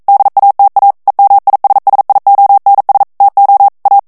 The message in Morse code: DATA EMISSIONS TO A